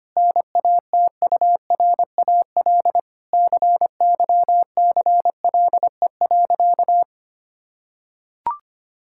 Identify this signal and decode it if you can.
Morse code, heard as a keyed tone, about 25 wpm, 700 Hz. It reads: NATURAL CYCLE.
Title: Sentences from Top 1000 Words - Review Part 6 - 25wpm